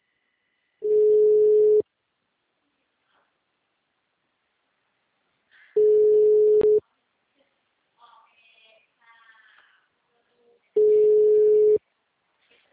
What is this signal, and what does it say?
Ringback tone (Europe, 425 Hz).